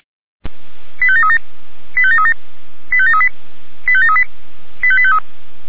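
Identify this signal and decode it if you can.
Five-tone selective call (ZVEI), heard as quick runs of five tones, tones of about 80 ms. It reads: 85737 85737 85738 85738 85853
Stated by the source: ZVEI five-tone selective call received on SDR.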